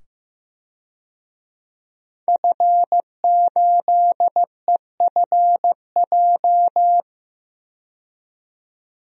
Morse code, heard as a keyed tone, about 15 wpm, 700 Hz. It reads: F8EFJ
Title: DXCC-100 Call Signs - 5 Characters - 15wpm